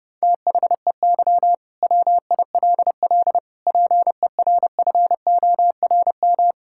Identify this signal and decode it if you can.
Morse code, heard as a keyed tone, about 32 wpm, 700 Hz. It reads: THEY WILL PERFORM
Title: Sentences from Top 1000 Words - Review Part 5 - 30wpm